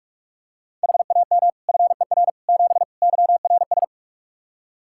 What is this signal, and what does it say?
Morse code, heard as a keyed tone, about 46 wpm, 700 Hz. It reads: HAM FER 7 YRS
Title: Practice QSO - Exchange 3 - More Info - 45wpm